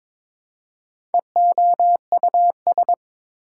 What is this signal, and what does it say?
Morse code, heard as a keyed tone, about 22 wpm, 700 Hz. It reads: EOUS
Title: Prefixes and Suffixes 22wpm